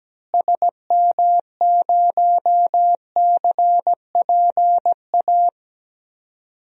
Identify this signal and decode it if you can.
Morse code, heard as a keyed tone, about 17 wpm, 700 Hz. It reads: SM0CPA
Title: DXCC-100 Call Signs - 6 Characters - Repeated 3x - 17wpm